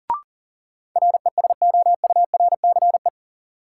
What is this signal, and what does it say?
Morse code, heard as a keyed tone, about 40 wpm, 700 Hz. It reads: RESOURCE
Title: Sentences from Top 1300 Words - New Word Drill Alt - 40wpm